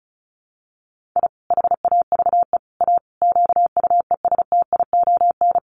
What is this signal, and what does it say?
Morse code, heard as a keyed tone, about 34 wpm, 700 Hz. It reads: I HAVE A QUESTION
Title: Sets of 5 Words Encore2 35wpm